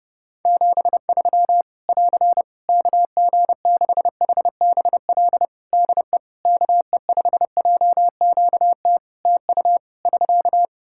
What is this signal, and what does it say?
Morse code, heard as a keyed tone, about 32 wpm, 700 Hz. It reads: 73 <AR> KG6HBL DE KE5JQT TU <SK>
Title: Practice QSO - Exchange 4 - Ending - 30wpm